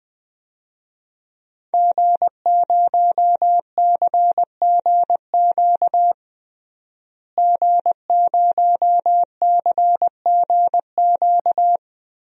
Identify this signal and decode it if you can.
Morse code, heard as a keyed tone, about 20 wpm, 700 Hz. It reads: G0CGQ G0CGQ
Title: DXCC-100 Call Signs - 5 Characters - Repeated 3x - 20wpm